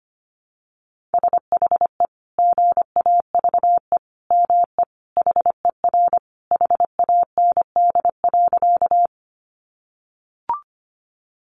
Morse code, heard as a keyed tone, about 25 wpm, 700 Hz. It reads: SHE GAVE ME HER HAND.
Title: Sentences from Top 500 Words 25wpm